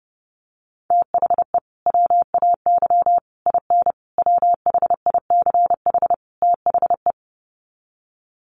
Morse code, heard as a keyed tone, about 30 wpm, 700 Hz. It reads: THE WAY IN WHICH THE
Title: Sets of 5 Words Encore2 30wpm